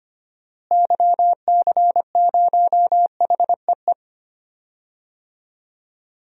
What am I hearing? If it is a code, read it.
Morse code, heard as a keyed tone, about 25 wpm, 700 Hz. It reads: YC0HEE
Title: DXCC-100 Call Signs - 6 Characters - Repeated 2x - 25wpm